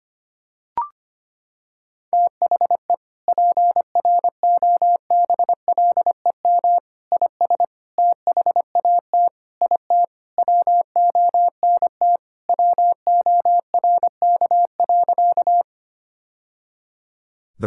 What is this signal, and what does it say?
Morse code, heard as a keyed tone, about 25 wpm, 700 Hz. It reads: THE PROBLEM IS THAT IT WONT WORK.
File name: practice / 25wpm / Sentences with 3-Word Phrases - 25wpm